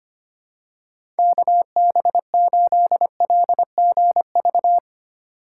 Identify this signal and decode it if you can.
Morse code, heard as a keyed tone, about 25 wpm, 700 Hz. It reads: KB8LGV
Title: USA Call Signs - 6 Characters - 4x Character Spacing - 25wpm